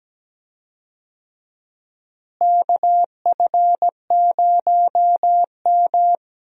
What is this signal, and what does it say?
Morse code, heard as a keyed tone, about 17 wpm, 700 Hz. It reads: KF0M
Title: USA Call Signs - 4 Characters - Repeated 2x - 17wpm